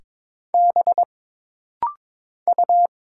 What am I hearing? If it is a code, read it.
Morse code, heard as a keyed tone, about 22 wpm, 700 Hz. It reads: B U
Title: Single Letters - Rapid-Fire 22wpm